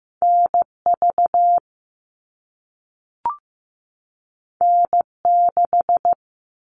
Morse code, heard as a keyed tone, about 15 wpm, 700 Hz. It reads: NV N6
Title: All Cut Numbers - 2-Digit 15wpm